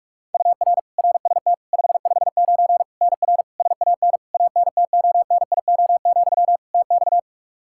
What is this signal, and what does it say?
Morse code, heard as a keyed tone, about 41 wpm, 700 Hz. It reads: UR RST 559 NR SAN ANTONIO, TX
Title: K1USN SST QSO Practice 45wpm